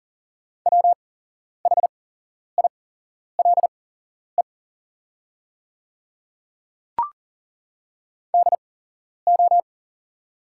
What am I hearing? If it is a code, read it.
Morse code, heard as a keyed tone, about 40 wpm, spaced as for 11 wpm, 690 Hz. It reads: WHILE DO
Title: 8x Character Spacing - Top 300 Words - 40wpm